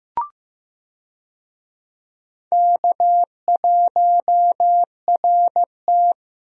Morse code, heard as a keyed tone, about 15 wpm, 700 Hz. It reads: K1RT
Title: USA Call Signs - 4 Characters - Repeated 3x - 15wpm